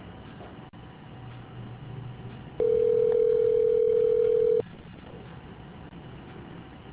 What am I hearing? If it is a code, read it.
Ringback tone (North America), 440 + 480 Hz.